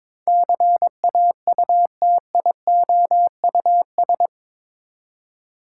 Morse code, heard as a keyed tone, about 21 wpm, 700 Hz. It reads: CAUTIOUS
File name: practice / 22wpm / Eight Letter Words - 22wpm